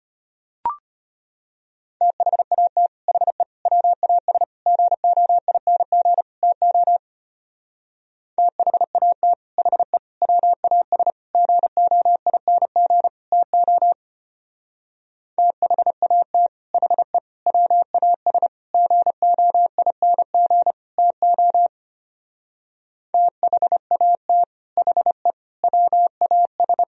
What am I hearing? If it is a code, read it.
Morse code, heard as a keyed tone, about 31 wpm, 700 Hz. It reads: THAT HE WAS GOING TO THAT HE WAS GOING TO THAT HE WAS GOING TO THAT HE WAS
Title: Speed-Racing - Sets of 5 Words 25wpm